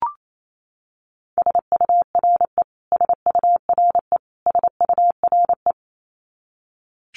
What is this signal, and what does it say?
Morse code, heard as a keyed tone, about 28 wpm, 700 Hz. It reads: SURE SURE SURE
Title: Top 500 Words Repeated 28wpm